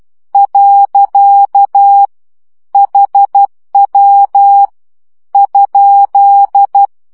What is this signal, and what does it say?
Morse code, heard as a keyed tone, about 12 wpm, 800 Hz. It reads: . HW ?
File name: morze vizsga 2011-3 10